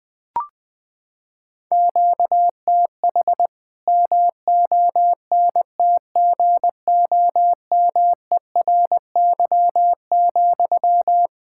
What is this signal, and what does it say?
Morse code, heard as a keyed tone, about 21 wpm, 700 Hz. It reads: QTH MONTGOMERY,
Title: K1USN SST QSO Practice 20wpm